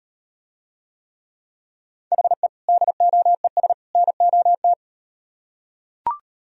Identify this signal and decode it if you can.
Morse code, heard as a keyed tone, about 38 wpm, 700 Hz. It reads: HE DOES NOT